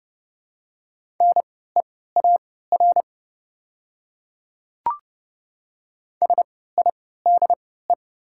Morse code, heard as a keyed tone, about 30 wpm, spaced as for 17 wpm, 700 Hz. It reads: NEAR SIDE
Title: 3x Character Spacing - Top 200 Words - 30wpm